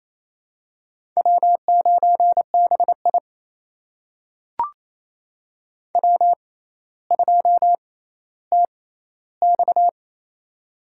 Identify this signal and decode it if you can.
Morse code, heard as a keyed tone, about 28 wpm, spaced as for 14 wpm, 700 Hz. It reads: W9BI W2TX
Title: USA Call Signs - 4 Characters - 6x Character Spacing - 28wpm